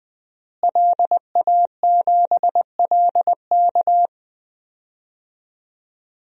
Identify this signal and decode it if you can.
Morse code, heard as a keyed tone, about 20 wpm, 700 Hz. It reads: LA7LK
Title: DXCC-100 Call Signs - 5 Characters - Repeated 2x - 20wpm